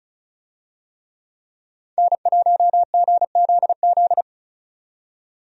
Morse code, heard as a keyed tone, about 36 wpm, 700 Hz. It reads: N1GZZ
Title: USA Call Signs - 5 Characters - 35wpm